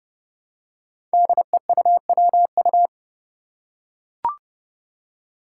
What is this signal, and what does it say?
Morse code, heard as a keyed tone, about 30 wpm, 700 Hz. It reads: DEUWU
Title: Five Challenging Letter-Number Combinations 30wpm